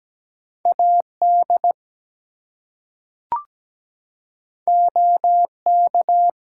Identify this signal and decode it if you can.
Morse code, heard as a keyed tone, about 17 wpm, 700 Hz. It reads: AD OK
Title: Two Letter Words - 17wpm